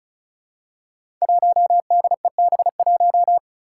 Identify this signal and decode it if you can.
Morse code, heard as a keyed tone, about 34 wpm, 700 Hz. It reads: 1DEB1
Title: Five Challenging Letter-Number Combinations 35wpm